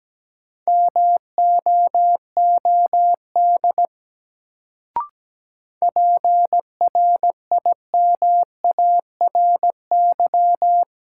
Morse code, heard as a keyed tone, about 17 wpm, 700 Hz. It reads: MOOD PRIMARY
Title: Sentences from Top 1100 Words - New Word Drill Alt - 17wpm